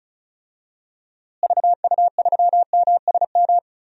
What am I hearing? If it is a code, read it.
Morse code, heard as a keyed tone, about 36 wpm, 700 Hz. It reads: VU3MSM
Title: DXCC-100 Call Signs - 6 Characters - Repeated 3x - 35wpm